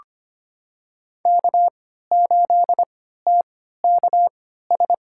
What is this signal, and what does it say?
Morse code, heard as a keyed tone, about 25 wpm, spaced as for 14 wpm, 700 Hz. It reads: K8TKS
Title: USA Call Signs - 5 Characters - 3x Character Spacing - 25wpm